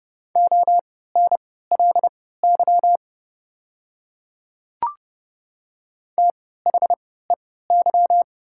Morse code, heard as a keyed tone, about 30 wpm, spaced as for 17 wpm, 690 Hz. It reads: ONLY THEY
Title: 3x Character Spacing - Top 500 Words - 30wpm